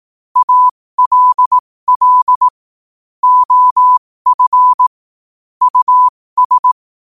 Morse code, heard as a keyed tone, about 18 wpm, spaced as for 15 wpm, 1000 Hz. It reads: ALL OF US
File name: qotdfiles / 15wpm